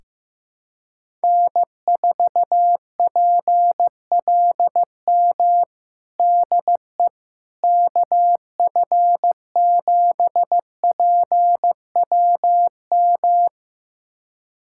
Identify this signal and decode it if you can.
Morse code, heard as a keyed tone, about 15 wpm, 700 Hz. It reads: N4PLM DE KF7PWM
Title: USA Callsign DE Callsign 15wpm